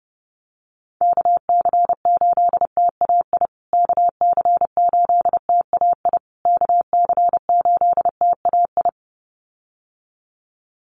Morse code, heard as a keyed tone, about 30 wpm, 700 Hz. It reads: KC8TAI KC8TAI KC8TAI
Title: USA Call Signs Repeated 3x 30wpm